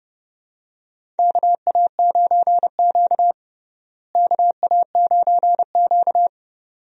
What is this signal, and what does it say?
Morse code, heard as a keyed tone, about 29 wpm, 700 Hz. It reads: KA9Q KA9Q
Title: USA Call Signs - 4 Characters - Repeated 2x - 30wpm